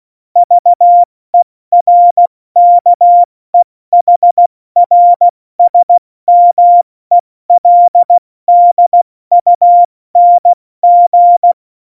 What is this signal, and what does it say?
Morse code, heard as a keyed tone, about 16 wpm, 700 Hz. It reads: VERKEHRSMELDUNG